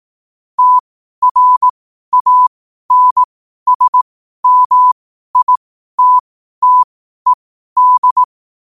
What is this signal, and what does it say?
Morse code, heard as a keyed tone, about 18 wpm, spaced as for 12 wpm, 1000 Hz. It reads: TRANSMITTED
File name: qotdfiles / 12wpm